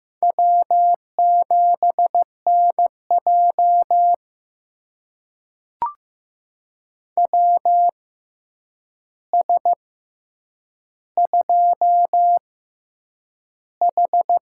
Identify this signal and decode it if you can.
Morse code, heard as a keyed tone, about 15 wpm, spaced as for 8 wpm, 700 Hz. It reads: W7NJ WS2H
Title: USA Call Signs - 4 Characters - 6x Character Spacing - 15wpm